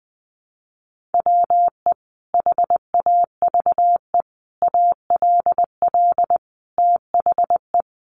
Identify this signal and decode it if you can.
Morse code, heard as a keyed tone, about 19 wpm, 700 Hz. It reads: WE HAVE ALL THE
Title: Sentences from Top 1300 Words 20wpm